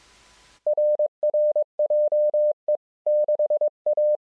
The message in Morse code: RR JE6A